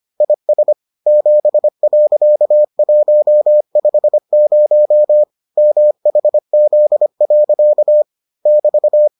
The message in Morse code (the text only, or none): IS 7.150 MHZ. <BT>